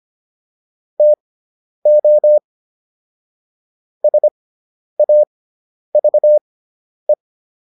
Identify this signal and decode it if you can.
Morse code, heard as a keyed tone, about 25 wpm, spaced as for 10 wpm, 600 Hz. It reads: TO SAVE